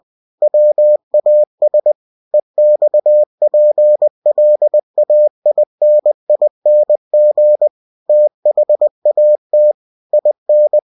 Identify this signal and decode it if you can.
Morse code, heard as a keyed tone, about 20 wpm, 600 Hz. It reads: WAS EXPLAINING THAT IN
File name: winnie-20wpm